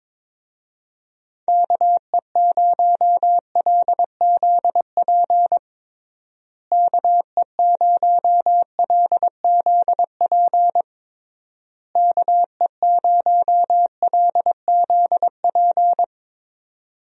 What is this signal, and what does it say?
Morse code, heard as a keyed tone, about 22 wpm, 700 Hz. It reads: KE0LZP KE0LZP KE0LZP
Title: USA Call Signs - 6 Characters - Repeated 3x - 22wpm